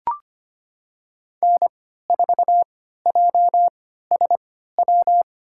Morse code, heard as a keyed tone, about 25 wpm, spaced as for 14 wpm, 700 Hz. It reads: N4JSW
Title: USA Call Signs - 5 Characters - 3x Character Spacing - 25wpm